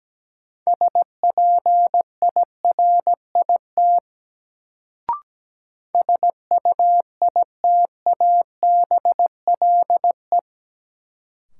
Morse code, heard as a keyed tone, about 17 wpm, 700 Hz. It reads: SPIRIT SUITABLE
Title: Sentences from Top 1400 Words - New Word Drill Alt - 17wpm